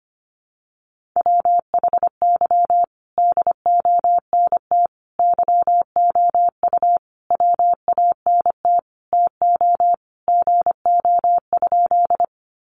Morse code, heard as a keyed tone, about 25 wpm, 700 Hz. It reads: WHY DONT YOU WANT TO GO?
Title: Sentences from Top 300 Words 25wpm